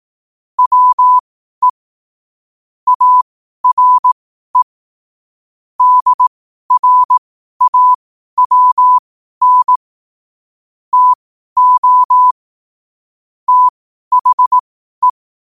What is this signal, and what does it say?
Morse code, heard as a keyed tone, about 18 wpm, spaced as for 12 wpm, 1000 Hz. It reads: WE ARE DRAWN TO THE
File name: qotdfiles / 12wpm